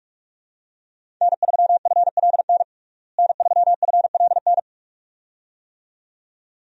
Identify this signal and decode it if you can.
Morse code, heard as a keyed tone, about 46 wpm, 700 Hz. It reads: N3FLN N3FLN
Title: USA Call Signs - 5 Characters - Repeated 2x - 45wpm